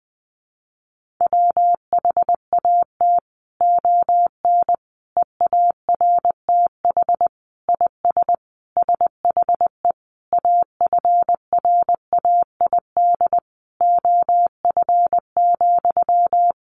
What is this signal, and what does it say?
Morse code, heard as a keyed tone, about 20 wpm, 700 Hz. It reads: WHAT ON EARTH IS SHE AFRAID OF,
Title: Sentences from Top 1500 Words 20wpm